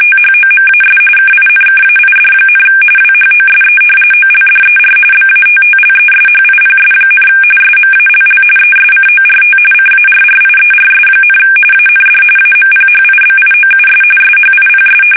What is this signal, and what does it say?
RTTY, about 45.45 baud. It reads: THOMAS KINKAID, WHOSE OLDER BATTLESHIPS, SMALL ESCORT CARRIERS, AND AMPHIBIOUS SHIPS
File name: fsk850 / battleshipnj